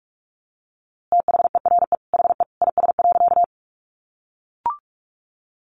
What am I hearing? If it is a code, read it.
Morse code, heard as a keyed tone, about 44 wpm, 700 Hz. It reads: THERE HE IS.
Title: A Sound Of Thunder 45wpm